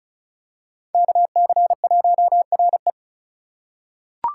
Morse code, heard as a keyed tone, about 35 wpm, 700 Hz. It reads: KC1RE
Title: USA Call Signs - 5 Characters - 2x Character Spacing - 35wpm